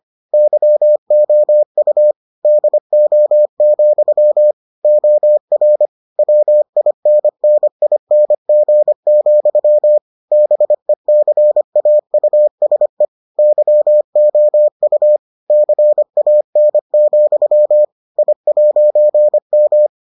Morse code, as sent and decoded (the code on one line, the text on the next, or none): -.-- --- ..- / -.. --- --..-- / --- .-. / .-- .. -. -. .. -. --. --..-- / -... -.-. .- ..- ... / -.-- --- ..- / -.-. .- -. --..-- / .. .----. --
YOU DO, OR WINNING, BECAUSE YOU CAN, I'M